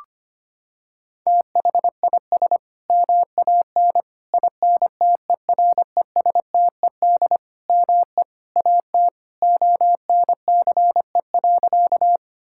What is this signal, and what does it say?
Morse code, heard as a keyed tone, about 25 wpm, 700 Hz. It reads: THIS MAN INTERESTED ME AT ONCE.
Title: Sentences from Top 1000 Words - Review Part 6 - 25wpm